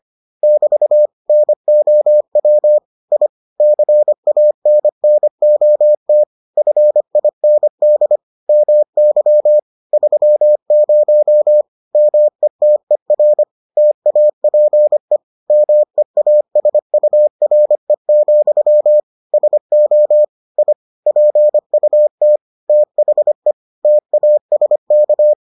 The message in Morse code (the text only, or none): <BT> NOW I CANNOT FIND MY 30 METER TAPE MEASURE, SO I PUT THE TASK